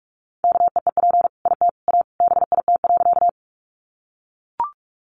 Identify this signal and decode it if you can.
Morse code, heard as a keyed tone, about 44 wpm, 700 Hz. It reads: KEEP IT A BIT.
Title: Sentences from Top 900 Words 45wpm